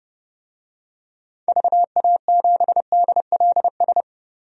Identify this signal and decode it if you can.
Morse code, heard as a keyed tone, about 30 wpm, 700 Hz. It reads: VA7DLS
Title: DXCC-100 Call Signs - 6 Characters - 30wpm